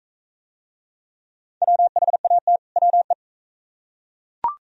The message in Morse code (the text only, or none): WHAT WE